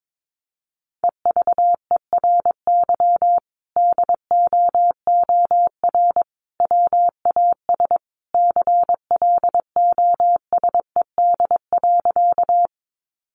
Morse code, heard as a keyed tone, about 22 wpm, 700 Hz. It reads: EVERY DOOR WAS CLOSED.
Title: Sentences from Top 1400 Words 22wpm